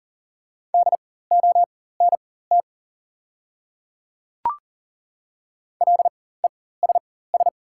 Morse code, heard as a keyed tone, about 39 wpm, spaced as for 19 wpm, 690 Hz. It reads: DONT LESS
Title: 4x Character Spacing - Top 500 Words - 40wpm